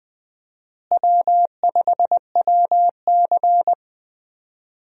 Morse code, heard as a keyed tone, about 20 wpm, 700 Hz. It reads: W5WC
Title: USA Call Signs - 4 Characters - Repeated 3x - 20wpm